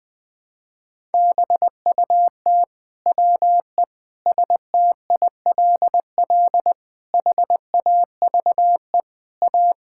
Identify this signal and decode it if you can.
Morse code, heard as a keyed tone, about 21 wpm, 700 Hz. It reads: BUT WE STILL HAVE A
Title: Sentences with 9-Word Phrases - 20wpm